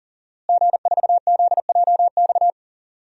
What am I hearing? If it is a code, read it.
Morse code, heard as a keyed tone, about 40 wpm, 700 Hz. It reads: G4ZJX
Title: DXCC-100 Call Signs - 5 Characters - Repeated 3x - 40wpm